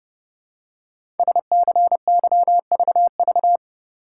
Morse code, heard as a keyed tone, about 30 wpm, 700 Hz. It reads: SCYVV
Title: Code Groups - 5 Letters - 30wpm